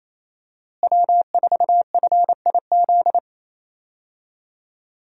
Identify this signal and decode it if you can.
Morse code, heard as a keyed tone, about 28 wpm, 700 Hz. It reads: W4FIZ
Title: USA Call Signs - 5 Characters - Repeated 2x - 28wpm